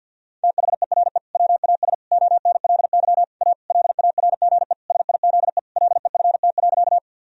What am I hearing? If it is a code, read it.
Morse code, heard as a keyed tone, about 52 wpm, 700 Hz. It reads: THERE WAS ONLY A LARGE SIZE LEFT.
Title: Sentences from Top 1000 Words - Review Part 2 - 50wpm